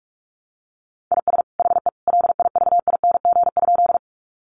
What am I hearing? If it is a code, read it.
Morse code, heard as a keyed tone, about 44 wpm, 700 Hz. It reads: IS HE LIVING?
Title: Sentences from Top 1100 Words 45wpm